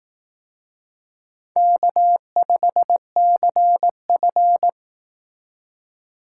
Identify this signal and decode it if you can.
Morse code, heard as a keyed tone, about 18 wpm, 700 Hz. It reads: K5CF
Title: Speed Racing - USA 1x2 Call Signs 15wpm